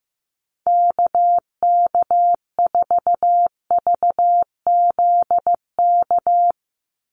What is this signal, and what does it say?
Morse code, heard as a keyed tone, about 15 wpm, 700 Hz. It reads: KK4VZK
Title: USA Call Signs 15wpm